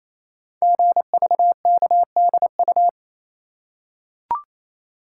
Morse code, heard as a keyed tone, about 27 wpm, 700 Hz. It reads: GVKDU
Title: Code Groups - 5 Letter:Numbers - 28wpm